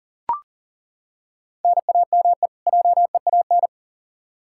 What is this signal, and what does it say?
Morse code, heard as a keyed tone, about 40 wpm, 700 Hz. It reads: NAME JEAN
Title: Practice QSO - Exchange 1 - Beginning - 40wpm